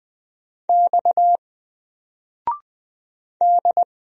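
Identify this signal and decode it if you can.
Morse code, heard as a keyed tone, about 20 wpm, 700 Hz. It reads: X D
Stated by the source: Single letter and number Morse drill at 20 wpm, from N4LCD.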